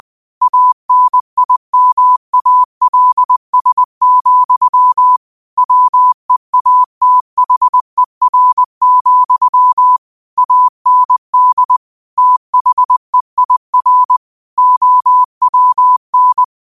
Morse code, heard as a keyed tone, about 20 wpm, 1000 Hz. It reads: ANIMALS, WEATHER, AND THEIR OWN
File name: qotdfiles / 20wpm